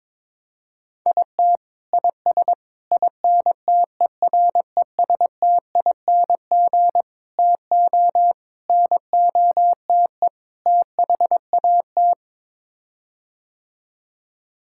Morse code, heard as a keyed tone, about 22 wpm, 700 Hz. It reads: IT IS INTERESTING TO NOTE THAT
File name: practice / 22wpm / Sets of 6 Words - 22wpm